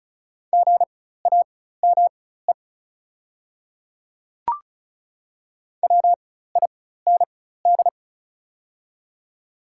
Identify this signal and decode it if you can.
Morse code, heard as a keyed tone, about 36 wpm, spaced as for 16 wpm, 700 Hz. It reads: GAME WIND
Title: 4x Character Spacing - Top 500 Words - 35wpm